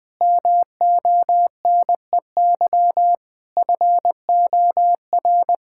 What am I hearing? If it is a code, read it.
Morse code, heard as a keyed tone, about 20 wpm, 700 Hz. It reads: MONEY FOR
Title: Sentences from Top 1000 Words - Review Part 6 - 20wpm